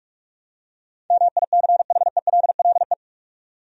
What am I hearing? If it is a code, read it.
Morse code, heard as a keyed tone, about 46 wpm, 700 Hz. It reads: MICHELLE
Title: USA Common Names - 45wpm